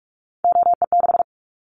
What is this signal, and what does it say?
Morse code, heard as a keyed tone, about 46 wpm, 700 Hz. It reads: OE6
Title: Callsign Prefixes for Top 50 DXCC - 45wpm